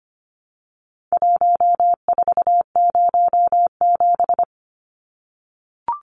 Morse code, heard as a keyed tone, about 25 wpm, 700 Hz. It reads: 1407
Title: Numbers - Contest Style - 4-Digit 25wpm